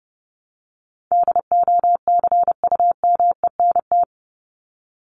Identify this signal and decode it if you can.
Morse code, heard as a keyed tone, about 30 wpm, 700 Hz. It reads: DOCUMENT
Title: Top 1751-2000 Words 30wpm